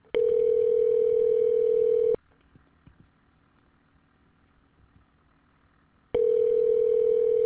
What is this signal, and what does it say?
Ringback tone (North America), 440 + 480 Hz.